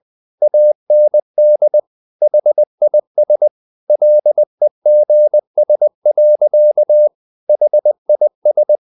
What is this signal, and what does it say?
Morse code, heard as a keyed tone, about 20 wpm, 600 Hz. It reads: AND HIS LEGS. HIS
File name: the old man and the sea part 2 20wpm